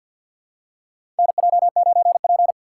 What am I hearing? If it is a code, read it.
Morse code, heard as a keyed tone, about 51 wpm, 700 Hz. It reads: NJ9P
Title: USA Call Signs - 3x Character Spacing - 50wpm